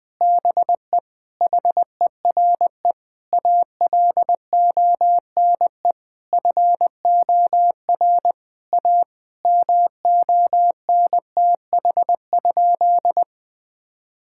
Morse code, heard as a keyed tone, about 20 wpm, 700 Hz. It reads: BE HERE ALONE FOR A MONTH?
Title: Sentences from Top 1000 Words - Review Part 5 - 20wpm